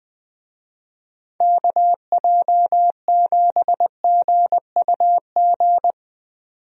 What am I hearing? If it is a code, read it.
Morse code, heard as a keyed tone, about 20 wpm, 700 Hz. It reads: KJ7GUG
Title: USA Call Signs - 6 Characters - 20wpm